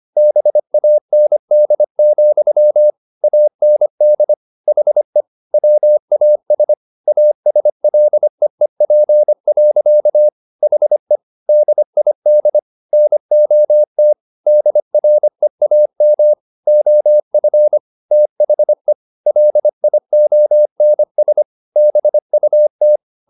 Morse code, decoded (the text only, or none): BAND, AND HE WAS ASLEEP. HE DID NOT DREAM OF THE LIONS BUT